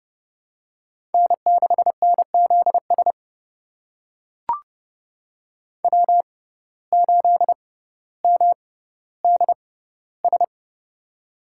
Morse code, heard as a keyed tone, about 30 wpm, spaced as for 15 wpm, 700 Hz. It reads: N6NZS W8MDS
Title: USA Call Signs - 6x Character Spacing - 30wpm